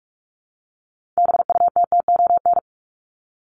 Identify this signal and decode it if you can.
Morse code, heard as a keyed tone, about 44 wpm, 700 Hz. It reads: BUTTON
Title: Top 1001-1250 Words 45wpm